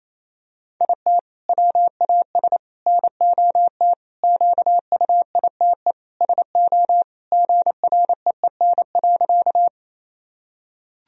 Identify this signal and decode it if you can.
Morse code, heard as a keyed tone, about 28 wpm, 700 Hz. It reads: IT WAS NOT QUITE SO GREEN.
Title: Sentences from Top 1000 Words - Review Part 2 - 28wpm